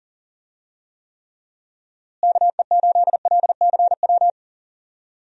Morse code, heard as a keyed tone, about 40 wpm, 700 Hz. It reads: KE8LCW
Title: USA Call Signs - 6 Characters - Repeated 2x - 40wpm